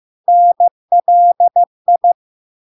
Morse code, heard as a keyed tone, about 15 wpm, 700 Hz. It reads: NLI
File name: LARK-FD-2022-Log-in-CW-15wpm-700-Hz